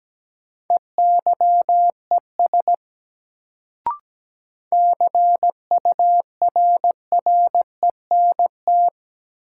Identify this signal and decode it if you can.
Morse code, heard as a keyed tone, about 17 wpm, 700 Hz. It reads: EYES CURRENT
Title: Sentences from Top 1000 Words - Review Part 4 - New Word Drill - 17wpm